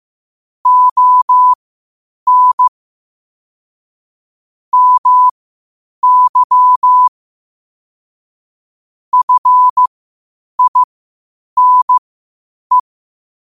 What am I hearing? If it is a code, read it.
Morse code, heard as a keyed tone, about 15 wpm, spaced as for 8 wpm, 1000 Hz. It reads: ON MY FINE
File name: qotdfiles / 8wpm